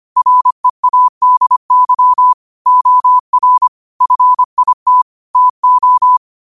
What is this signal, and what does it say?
Morse code, heard as a keyed tone, about 25 wpm, 1000 Hz. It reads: READY OR FIT TO
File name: qotdfiles / 25wpm